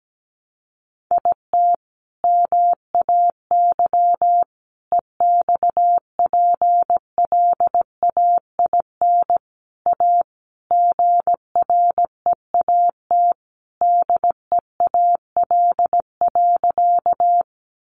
Morse code, heard as a keyed tone, about 17 wpm, 700 Hz. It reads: IT MAY EXPLAIN A GREAT DEAL.
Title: Sentences from Top 1100 Words 17wpm